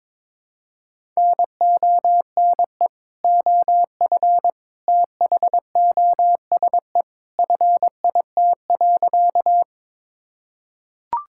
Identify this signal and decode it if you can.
Morse code, heard as a keyed tone, about 22 wpm, 700 Hz. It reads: NONE OF THOSE FIT.
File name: practice / 22wpm / Sentences with 2-Word Phrases - 22wpm